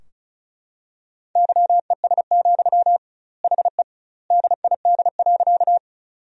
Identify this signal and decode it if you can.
Morse code, heard as a keyed tone, about 34 wpm, 700 Hz. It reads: YES, HE DID.
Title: 2x Word-Spacing - Sentences from Top 500 Words - 35wpm